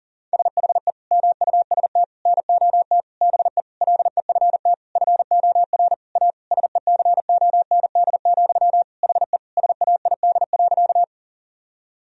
Morse code, heard as a keyed tone, about 40 wpm, 700 Hz. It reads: SHE MUST NOT BE LEFT FOR A SECOND, HE SAID.
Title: Sentences from Top 1000 Words - Review Part 1 - 40wpm